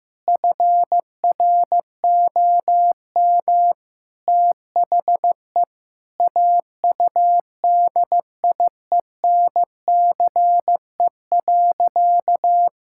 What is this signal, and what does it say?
Morse code, heard as a keyed tone, about 16 wpm, 700 Hz. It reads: FROM THE AUDIENCE.
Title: Sentences from Top 1000 Words - Review Part 5 - 15wpm